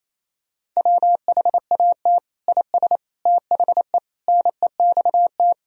Morse code, heard as a keyed tone, about 28 wpm, 700 Hz. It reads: WHAT IS THE NEXT
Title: Sentences from Top 1000 Words - Review Part 1 - 28wpm